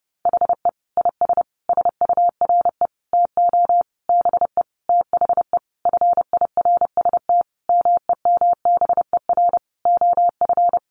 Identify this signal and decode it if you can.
Morse code, heard as a keyed tone, about 30 wpm, 700 Hz. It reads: HE IS SURE TO BE THE FIRST MEMBER OF
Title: Sentences from Top 1400 Words 30wpm